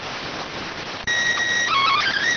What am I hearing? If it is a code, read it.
SSTV (slow-scan television), PD 290.